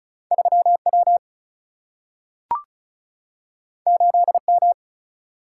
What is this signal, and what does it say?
Morse code, heard as a keyed tone, about 36 wpm, 700 Hz. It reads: 3 W 8 M
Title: Single-Letter Single-Number 35wpm